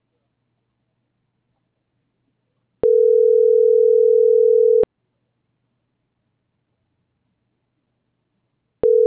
Ringback tone (North America), 440 + 480 Hz.